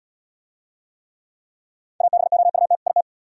Morse code, heard as a keyed tone, about 74 wpm, 700 Hz. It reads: ABOUT IT